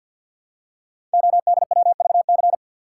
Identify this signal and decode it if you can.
Morse code, heard as a keyed tone, about 45 wpm, 700 Hz. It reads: ODWVC
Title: Code Groups - 5 Letters - 50wpm